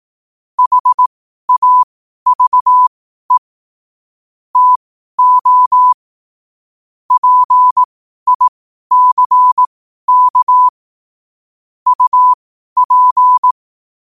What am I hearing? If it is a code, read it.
Morse code, heard as a keyed tone, about 18 wpm, spaced as for 12 wpm, 1000 Hz. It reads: HAVE TO PICK UP
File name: qotdfiles / 12wpm